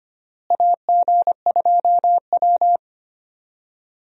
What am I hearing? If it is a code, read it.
Morse code, heard as a keyed tone, about 25 wpm, 700 Hz. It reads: AG2W
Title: USA Call Signs - 4 Characters - Repeated 3x - 25wpm